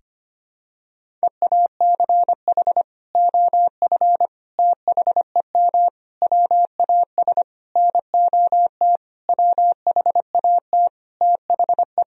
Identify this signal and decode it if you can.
Morse code, heard as a keyed tone, about 25 wpm, 700 Hz. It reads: EACH OF THEM WAS NOT WHAT THE
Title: Sentences from Top 1000 Words - Review Part 1 - 25wpm